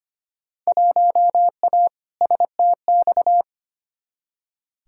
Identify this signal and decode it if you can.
Morse code, heard as a keyed tone, about 25 wpm, 700 Hz. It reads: 1A STX
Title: Field Day Exchanges - 25wpm